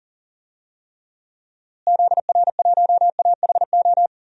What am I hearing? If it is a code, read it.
Morse code, heard as a keyed tone, about 40 wpm, 700 Hz. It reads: ZR1AHO
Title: DXCC-100 Call Signs - 6 Characters - Repeated 2x - 40wpm